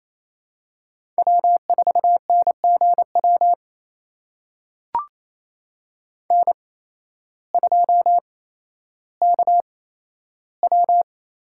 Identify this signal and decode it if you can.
Morse code, heard as a keyed tone, about 28 wpm, spaced as for 13 wpm, 700 Hz. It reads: W4NGW N2KW
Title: USA Call Signs - 5 Characters - 8x Character Spacing - 28wpm